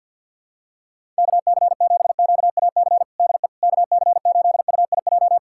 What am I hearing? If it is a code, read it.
Morse code, heard as a keyed tone, about 52 wpm, 700 Hz. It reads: KC7QAC DE KC8UIJ
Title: USA Callsign DE Callsign - 6 Characters - 50wpm